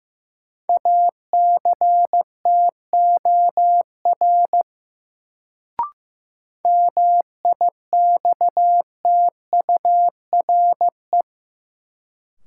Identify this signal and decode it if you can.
Morse code, heard as a keyed tone, about 15 wpm, 700 Hz. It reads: ACTOR MIXTURE
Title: Sentences from Top 1500 Words - New Word Drill - 15wpm